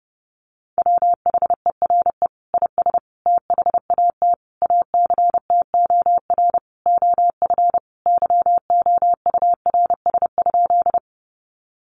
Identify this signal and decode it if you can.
Morse code, heard as a keyed tone, about 30 wpm, 700 Hz. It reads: WHERE IS THAT ACTOR OF YOURS?
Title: Sentences from Top 1500 Words 30wpm